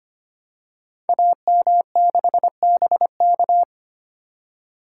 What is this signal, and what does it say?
Morse code, heard as a keyed tone, about 25 wpm, 700 Hz. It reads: AM6BK
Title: Code Groups - 5 Letter:Numbers - 25wpm